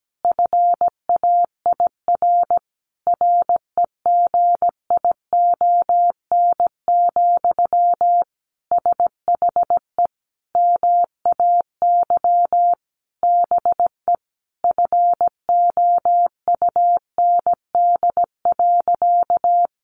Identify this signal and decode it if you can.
Morse code, heard as a keyed tone, about 17 wpm, 700 Hz. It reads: FAIR REGION, SHE MAY BE FOUND.
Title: Sentences from Top 1500 Words 17wpm